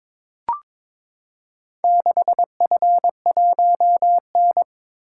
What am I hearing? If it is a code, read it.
Morse code, heard as a keyed tone, about 23 wpm, 700 Hz. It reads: 6F1N
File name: practice / 22wpm / Four Letter-Number Combinations - 22wpm